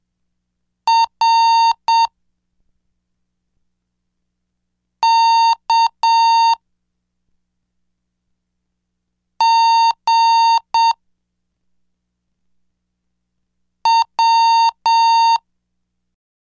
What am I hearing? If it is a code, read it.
Morse code, heard as a keyed tone, about 7 wpm, 920 Hz. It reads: R K G W
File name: cursos / CDestudo / 10 Track 10